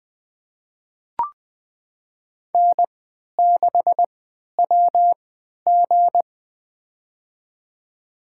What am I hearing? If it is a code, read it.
Morse code, heard as a keyed tone, about 21 wpm, spaced as for 12 wpm, 700 Hz. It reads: N6WG
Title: USA Call Signs - 4 Characters - 3x Character Spacing - 20wpm